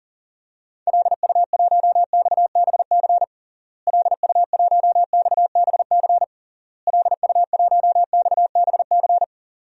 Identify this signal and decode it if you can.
Morse code, heard as a keyed tone, about 40 wpm, 700 Hz. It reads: LU1XBC LU1XBC LU1XBC
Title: DXCC-100 Call Signs - 6 Characters - Repeated 3x - 40wpm